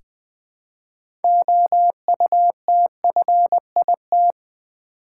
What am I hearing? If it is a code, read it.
Morse code, heard as a keyed tone, about 20 wpm, 700 Hz. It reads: OUTFIT
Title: 6 Letter Words - First 3 Letters Given - 20wpm